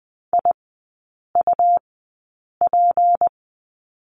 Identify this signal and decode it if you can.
Morse code, heard as a keyed tone, about 20 wpm, spaced as for 8 wpm, 700 Hz. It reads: IUP
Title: 152 Lesson 25e - Letter p - Three Characters 20wpm